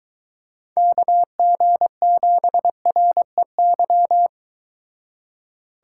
Morse code, heard as a keyed tone, about 23 wpm, 700 Hz. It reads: KG7REY